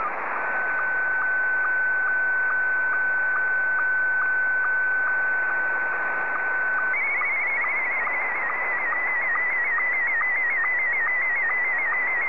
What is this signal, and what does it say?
SSTV (slow-scan television), Scottie 1.
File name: sstv / S1-VK7ZCF